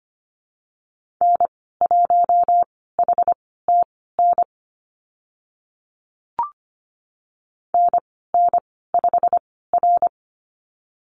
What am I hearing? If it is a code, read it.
Morse code, heard as a keyed tone, about 25 wpm, spaced as for 16 wpm, 690 Hz. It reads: N1HTN NN5R
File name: Lesson 15g - Number 5 - Call Signs 25wpm